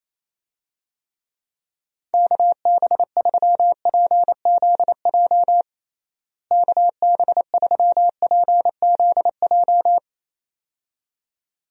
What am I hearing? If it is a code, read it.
Morse code, heard as a keyed tone, about 27 wpm, 700 Hz. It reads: KB3PZJ KB3PZJ
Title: USA Call Signs - 6 Characters - Repeated 2x - 28wpm